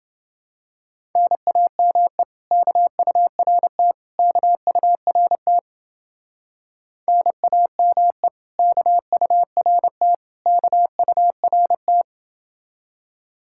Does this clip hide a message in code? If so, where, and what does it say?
Yes: Morse code from 1.2 to 12.0 s, heard as a keyed tone, about 29 wpm, 700 Hz. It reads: NAME KURT KURT NAME KURT KURT